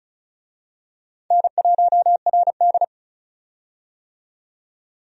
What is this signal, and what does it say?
Morse code, heard as a keyed tone, about 36 wpm, 700 Hz. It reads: N1RD
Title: USA Call Signs - 4 Characters - 35wpm